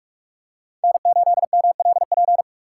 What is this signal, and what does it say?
Morse code, heard as a keyed tone, about 47 wpm, 700 Hz. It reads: N8MLP
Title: USA Call Signs - 5 Characters - 2x Character Spacing - 45wpm